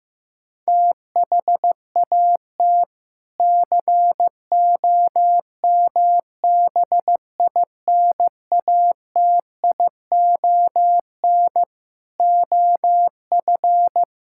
Morse code, heard as a keyed tone, about 15 wpm, 700 Hz. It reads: THAT COMBINATION OF
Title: Sentences from Top 1000 Words - Review Part 6 - 15wpm